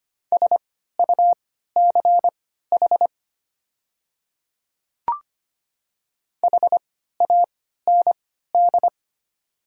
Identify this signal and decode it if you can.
Morse code, heard as a keyed tone, about 25 wpm, spaced as for 14 wpm, 700 Hz. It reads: SUCH HAND
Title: 3x Character Spacing - Top 200 Words - 25wpm